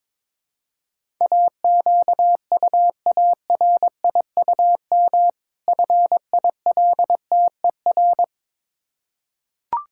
Morse code, heard as a keyed tone, about 22 wpm, 700 Hz. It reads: AQUARIUM FILTER
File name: Difficult Words with Hints - 22wpm